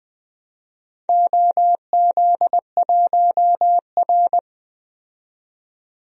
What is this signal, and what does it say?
Morse code, heard as a keyed tone, about 20 wpm, 700 Hz. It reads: OZ1R
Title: DXCC-100 Call Signs - 4 Characters - 20wpm